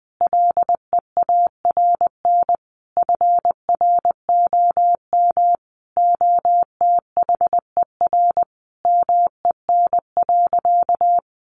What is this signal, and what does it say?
Morse code, heard as a keyed tone, about 19 wpm, 700 Hz. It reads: LEARN FROM OTHER MEN.
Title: Sentences from Top 1100 Words 20wpm